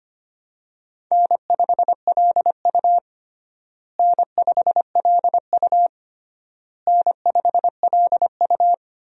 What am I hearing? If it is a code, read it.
Morse code, heard as a keyed tone, about 25 wpm, 700 Hz. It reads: N5LU N5LU N5LU
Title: USA Call Signs - 4 Characters - Repeated 3x - 25wpm